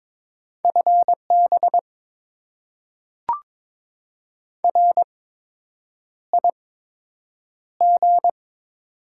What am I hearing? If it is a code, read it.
Morse code, heard as a keyed tone, about 22 wpm, spaced as for 8 wpm, 700 Hz. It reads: FB RIG
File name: Top 20 QSO Elements - 8x Character Spacing - 22wpm